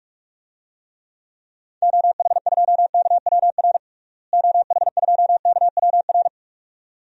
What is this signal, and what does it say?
Morse code, heard as a keyed tone, about 47 wpm, 700 Hz. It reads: OH2KWR OH2KWR
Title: DXCC-100 Call Signs - 6 Characters - Repeated 2x - 45wpm